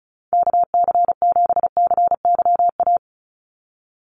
Morse code, heard as a keyed tone, about 34 wpm, 700 Hz. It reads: KC7CYA
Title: USA Call Signs 35wpm